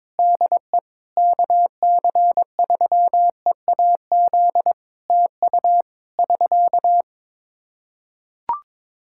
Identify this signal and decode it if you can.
Morse code, heard as a keyed tone, about 21 wpm, 700 Hz. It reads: DE KC3EAZ TU <SK>
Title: Practice QSO - Exchange 4 - Ending - 22wpm